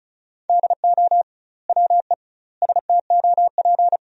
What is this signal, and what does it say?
Morse code, heard as a keyed tone, about 36 wpm, 700 Hz. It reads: DO WE STOP
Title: 2x Word-Spacing - Sentences from Top 300 Words - 35wpm